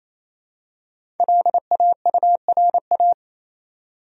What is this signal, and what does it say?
Morse code, heard as a keyed tone, about 28 wpm, 700 Hz. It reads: LAURA
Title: USA Common Names - 28wpm